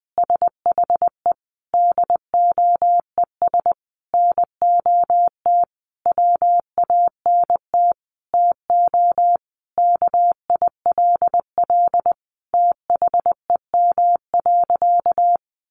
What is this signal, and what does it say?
Morse code, heard as a keyed tone, about 20 wpm, 700 Hz. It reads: SHE DOES NOT WANT TO KILL THEM.
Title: Additional Easy Phrases 20wpm